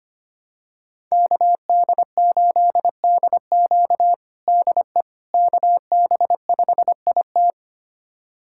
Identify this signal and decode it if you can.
Morse code, heard as a keyed tone, about 25 wpm, 700 Hz. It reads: KD8DQ DE KB5IT
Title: USA Callsign DE Callsign - 5 Characters - 25wpm